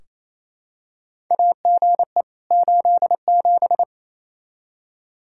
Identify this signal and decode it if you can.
Morse code, heard as a keyed tone, about 28 wpm, 700 Hz. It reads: AGE 87
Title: K1USN SST QSO Practice 28wpm